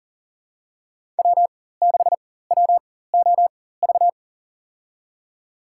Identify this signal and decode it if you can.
Morse code, heard as a keyed tone, about 40 wpm, spaced as for 19 wpm, 700 Hz. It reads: W6WOV
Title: USA Call Signs - 5 Characters - 4x Character Spacing - 40wpm